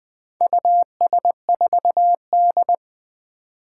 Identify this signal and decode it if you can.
Morse code, heard as a keyed tone, about 20 wpm, 700 Hz. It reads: US4D
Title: Four Challenging Letter-Number Combinations 20wpm